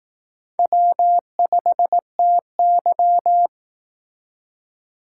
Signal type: Morse code

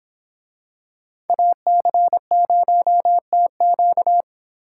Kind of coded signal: Morse code

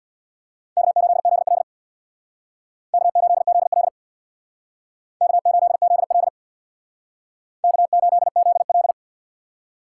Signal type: Morse code

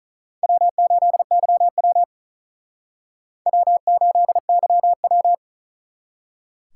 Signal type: Morse code

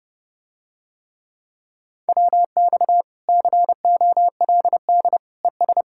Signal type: Morse code

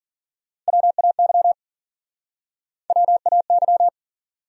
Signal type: Morse code

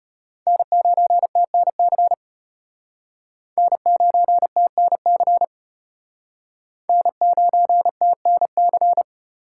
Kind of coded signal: Morse code

